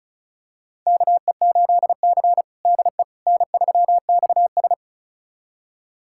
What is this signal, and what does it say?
Morse code, heard as a keyed tone, about 34 wpm, 700 Hz. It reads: KE8C DE N3XS
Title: USA Callsign DE Callsign - 4 Characters - 35wpm